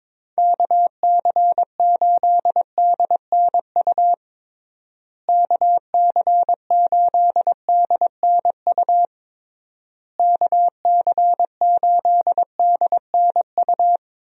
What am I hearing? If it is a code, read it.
Morse code, heard as a keyed tone, about 22 wpm, 700 Hz. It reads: KC8DNU KC8DNU KC8DNU
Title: USA Call Signs - 6 Characters - Repeated 3x - 22wpm